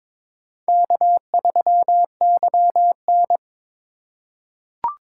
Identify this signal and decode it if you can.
Morse code, heard as a keyed tone, about 22 wpm, 700 Hz. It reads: K3YN
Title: USA Call Signs - 3x Character Spacing - 22wpm